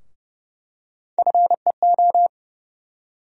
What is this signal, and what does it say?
Morse code, heard as a keyed tone, about 32 wpm, 700 Hz. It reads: FEO
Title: USA Call Sign Postfixes - 3 Letters 30wpm